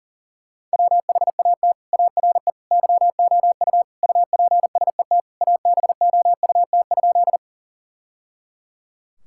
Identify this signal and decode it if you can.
Morse code, heard as a keyed tone, about 37 wpm, 700 Hz. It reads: WHAT ARE YOU UPSET ABOUT?
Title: Sentences from Top 1000 Words - Review Part 5 - 40wpm